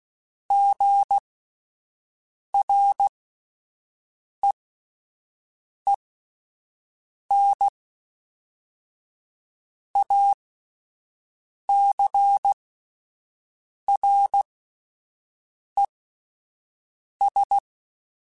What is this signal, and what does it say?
Morse code, heard as a keyed tone, about 16 wpm, spaced as for 5 wpm, 800 Hz. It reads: GREENACRES
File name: fromSMs / FromSMs mp3s / Cities@5